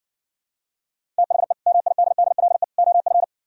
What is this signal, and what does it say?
Morse code, heard as a keyed tone, about 60 wpm, 700 Hz. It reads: THE MIDDLE OF